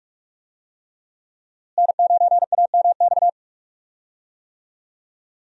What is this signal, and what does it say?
Morse code, heard as a keyed tone, about 46 wpm, 700 Hz. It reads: N9AMX
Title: USA Call Signs - 5 Characters - Repeated 3x - 45wpm